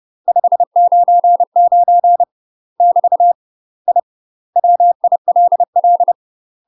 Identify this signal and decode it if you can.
Morse code, heard as a keyed tone, about 30 wpm, 700 Hz. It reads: 599 <BT> I WILL
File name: UK MASTER QSOs TRAINING / 30WPM / UK MASTER QSOs 0013